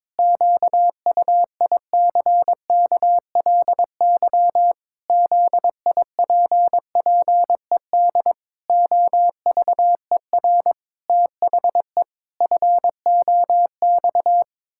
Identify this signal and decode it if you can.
Morse code, heard as a keyed tone, about 22 wpm, 700 Hz. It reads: QUICKLY ZIPPED OVER THE FOX
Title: Pangram Challenge v2 - 22wpm